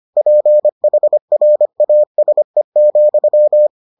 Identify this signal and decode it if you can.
Morse code, heard as a keyed tone, about 25 wpm, 600 Hz. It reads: PHRASE,